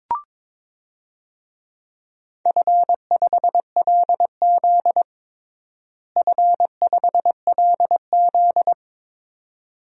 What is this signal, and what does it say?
Morse code, heard as a keyed tone, about 22 wpm, 700 Hz. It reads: F5LZ F5LZ
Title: DXCC-100 Call Signs - 4 Characters - Repeated 2x - 22wpm